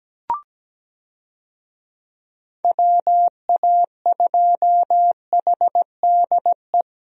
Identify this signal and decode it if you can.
Morse code, heard as a keyed tone, about 17 wpm, 700 Hz. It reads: WA2HDE
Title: USA Call Signs - 6 Characters - 17wpm